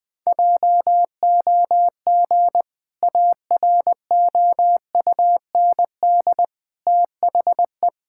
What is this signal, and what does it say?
Morse code, heard as a keyed tone, about 21 wpm, 700 Hz. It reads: JOG AROUND THE
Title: Easy Sentences with Hints - 20wpm